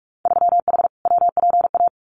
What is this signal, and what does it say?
Morse code, heard as a keyed tone, about 46 wpm, 700 Hz. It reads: 3H WPA
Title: Winter Field Day Exchanges 45wpm